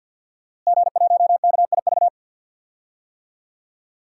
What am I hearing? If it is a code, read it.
Morse code, heard as a keyed tone, about 52 wpm, 700 Hz. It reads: G1KIV
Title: DXCC-100 Call Signs - 5 Characters - Repeated 2x - 50wpm